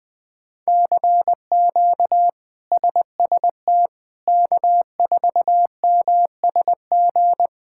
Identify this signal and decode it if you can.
Morse code, heard as a keyed tone, about 21 wpm, 700 Hz. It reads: CQ SST K4MSG
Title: K1USN SST QSO Practice - 20wpm